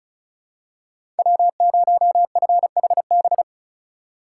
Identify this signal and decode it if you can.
Morse code, heard as a keyed tone, about 35 wpm, 700 Hz. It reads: W0FHB
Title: USA Call Signs - 5 Characters - 2x Character Spacing - 35wpm